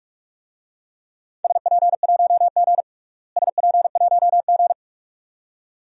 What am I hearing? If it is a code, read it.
Morse code, heard as a keyed tone, about 41 wpm, 700 Hz. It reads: SP1G SP1G
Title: DXCC-100 Call Signs - 4 Characters - Repeated 2x - 45wpm